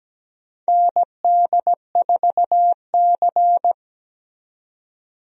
Morse code, heard as a keyed tone, about 17 wpm, 700 Hz. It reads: ND4C
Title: USA Call Signs - 4 Characters - 17wpm